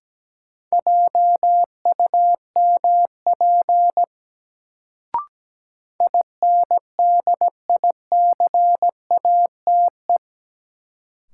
Morse code, heard as a keyed tone, about 17 wpm, 700 Hz. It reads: JUMP INDICATE
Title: Sentences from Top 1300 Words - New Word Drill - 17wpm